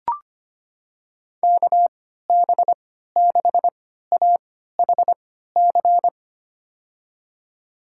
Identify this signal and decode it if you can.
Morse code, heard as a keyed tone, about 25 wpm, spaced as for 14 wpm, 700 Hz. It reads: KB6AHC
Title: USA Call Signs - 6 Characters - 3x Character Spacing - 25wpm